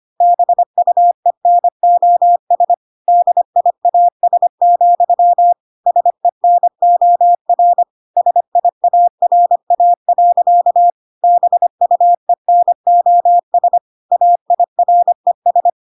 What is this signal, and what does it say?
Morse code, heard as a keyed tone, about 25 wpm, 700 Hz. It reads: BUENOS DIAS, SENOR SIARA. BUENOS AIRES